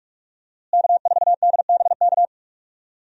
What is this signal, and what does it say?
Morse code, heard as a keyed tone, about 42 wpm, 700 Hz. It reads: K4DBK
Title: USA Call Signs - 5 Characters - Repeated 2x - 45wpm